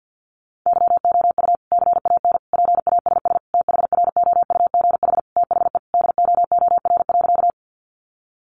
Morse code, heard as a keyed tone, about 50 wpm, 700 Hz. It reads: YOU CAN PASS THROUGH THE DOOR.
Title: Sentences from Top 400 Words 50wpm